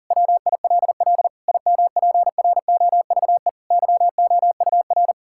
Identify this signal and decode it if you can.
Morse code, heard as a keyed tone, about 40 wpm, 700 Hz. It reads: WILL IMPROVE YOUR
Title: Sentences from Top 1000 Words - Review Part 6 - 40wpm